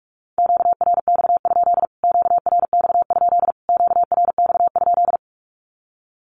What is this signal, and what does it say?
Morse code, heard as a keyed tone, about 44 wpm, 700 Hz. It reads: QRX? QRX? QRX?
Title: Q-Codes Repeated 3x 45wpm